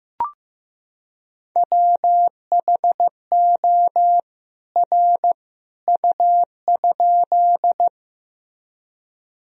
Morse code, heard as a keyed tone, about 15 wpm, 700 Hz. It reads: WHO R U?
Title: Sentences from Top 100 Words and Abbreviations 15wpm